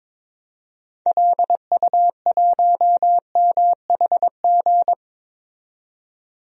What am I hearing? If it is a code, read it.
Morse code, heard as a keyed tone, about 22 wpm, 700 Hz. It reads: LU1MHG
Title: DXCC-100 Call Signs - 6 Characters - 22wpm